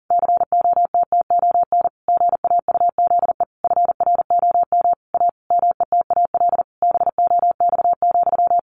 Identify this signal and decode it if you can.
Morse code, heard as a keyed tone, about 40 wpm, 700 Hz. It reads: COTTON GAUZE FROM A METAL BOX,
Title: A Sound Of Thunder 40wpm